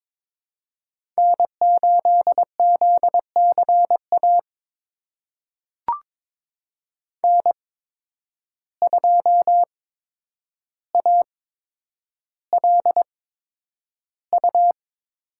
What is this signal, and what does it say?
Morse code, heard as a keyed tone, about 22 wpm, spaced as for 9 wpm, 700 Hz. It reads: N8ZCA N2ALU
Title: USA Call Signs - 5 Characters - 8x Character Spacing - 22wpm